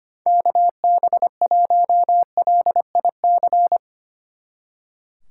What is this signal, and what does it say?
Morse code, heard as a keyed tone, about 32 wpm, 700 Hz. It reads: KB1LIC